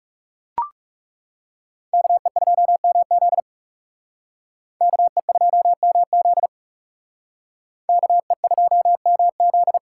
Morse code, heard as a keyed tone, about 37 wpm, 700 Hz. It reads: KE2MZ KE2MZ KE2MZ
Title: Speed Racing - USA 2x2 Call Signs 30wpm